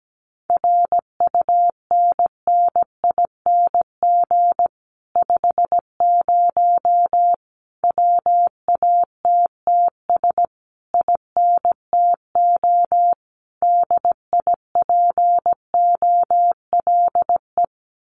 Morse code, heard as a keyed tone, about 17 wpm, 700 Hz. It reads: RUNNING 50 WATTS INTO DIPOLE
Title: Common QSO Phrases Encore 17wpm